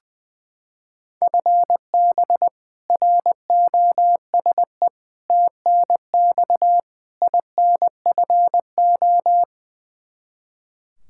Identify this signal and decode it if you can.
Morse code, heard as a keyed tone, about 20 wpm, 700 Hz. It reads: FB ROSE TNX INFO
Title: Practice QSO - Exchange 3 - More Info - 20wpm